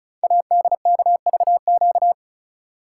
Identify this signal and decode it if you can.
Morse code, heard as a keyed tone, about 36 wpm, 700 Hz. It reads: ADKVQ
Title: Code Groups - 5 Letters - 35wpm